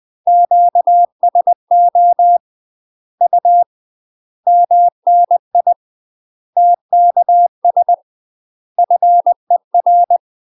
Morse code, heard as a keyed tone, about 20 wpm, 700 Hz. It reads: QSO U MNI TKS FER
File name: UK MASTER QSOs TRAINING / 20WPM / UK MASTER QSOs 0007